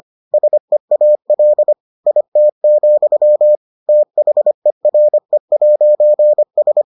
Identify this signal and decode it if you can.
Morse code, heard as a keyed tone, about 25 wpm, 600 Hz. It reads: SEAL IT, THERE'S